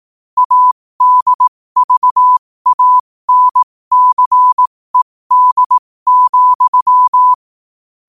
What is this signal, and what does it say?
Morse code, heard as a keyed tone, about 18 wpm, spaced as for 15 wpm, 1000 Hz. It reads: ADVANCED,
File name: qotdfiles / 15wpm